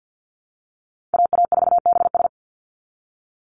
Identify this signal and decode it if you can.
Morse code, heard as a keyed tone, about 50 wpm, 700 Hz. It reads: AA4BS
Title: USA Call Signs 50wpm